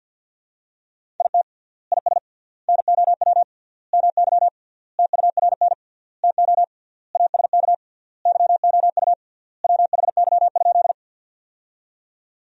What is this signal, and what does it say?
Morse code, heard as a keyed tone, about 50 wpm, 700 Hz. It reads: IT IS NOW MY TURN TO ASK YOU WHY?
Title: 3x Word-Spacing - Sentences from Top 200 Words - 50wpm